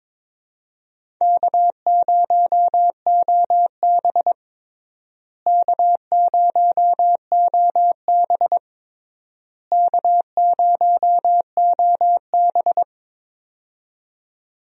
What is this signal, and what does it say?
Morse code, heard as a keyed tone, about 22 wpm, 700 Hz. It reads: K0OB K0OB K0OB
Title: USA Call Signs - 4 Characters - Repeated 3x - 22wpm